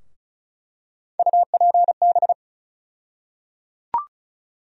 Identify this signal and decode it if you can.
Morse code, heard as a keyed tone, about 34 wpm, 700 Hz. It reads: UPB
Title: USA Call Sign Postfixes - 3 Letters 35wpm